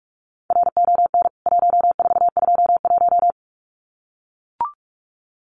Morse code, heard as a keyed tone, about 44 wpm, 700 Hz. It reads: RON 1421
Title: CWT Exchange 45wpm